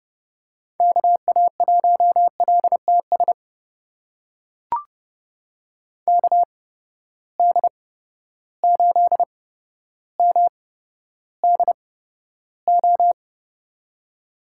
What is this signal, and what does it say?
Morse code, heard as a keyed tone, about 32 wpm, spaced as for 13 wpm, 700 Hz. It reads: KA1LTS KD8MDO
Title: USA Call Signs - 6 Characters - 8x Character Spacing - 30wpm